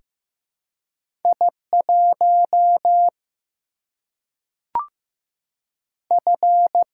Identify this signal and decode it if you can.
Morse code, heard as a keyed tone, about 15 wpm, 700 Hz. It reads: I 1 F
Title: Single-Letter Single-Number 15wpm